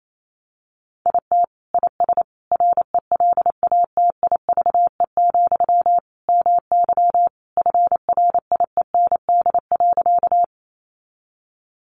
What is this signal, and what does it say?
Morse code, heard as a keyed tone, about 28 wpm, 700 Hz. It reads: IT IS RELATIVE, MY FRIEND.
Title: Sentences from Top 1400 Words 28wpm